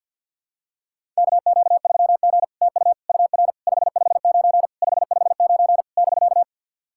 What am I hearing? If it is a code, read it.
Morse code, heard as a keyed tone, about 48 wpm, 700 Hz. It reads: KQ3G TU UR 559 559 <BK>
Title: SOTA QSO Practice 50wpm